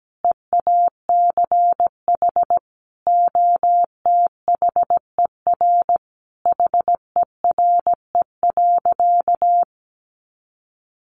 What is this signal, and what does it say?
Morse code, heard as a keyed tone, about 17 wpm, 700 Hz. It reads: EACH OTHER HERE.
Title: Sentences from Top 1200 Words 17wpm